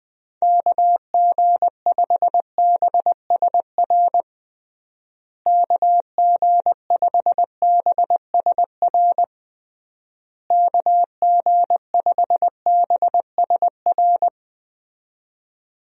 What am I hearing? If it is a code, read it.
Morse code, heard as a keyed tone, about 20 wpm, 700 Hz. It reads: KG5BSR KG5BSR KG5BSR
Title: USA Call Signs - 6 Characters - Repeated 3x - 20wpm